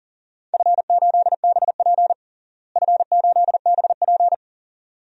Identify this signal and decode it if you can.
Morse code, heard as a keyed tone, about 40 wpm, 700 Hz. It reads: F8BP F8BP
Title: DXCC-100 Call Signs - 4 Characters - Repeated 2x - 40wpm